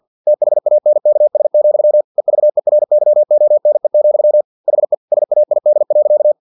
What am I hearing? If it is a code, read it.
Morse code, heard as a keyed tone, about 50 wpm, 600 Hz. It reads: THANKS, EVERYONE, HE SAID.
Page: Never 49wpm